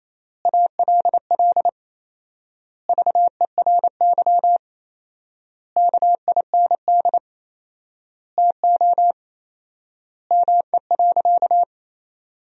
Morse code, heard as a keyed tone, about 28 wpm, 700 Hz. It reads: ALL VERY KIND TO ME.
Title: 4x Word-Spacing - Sentences from Top 200 Words - 28wpm